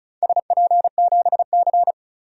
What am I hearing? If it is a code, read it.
Morse code, heard as a keyed tone, about 36 wpm, 700 Hz. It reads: SP7C
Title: DXCC-100 Call Signs - 4 Characters - 35wpm